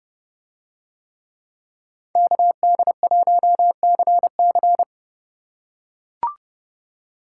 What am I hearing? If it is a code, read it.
Morse code, heard as a keyed tone, about 29 wpm, 700 Hz. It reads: KD1CC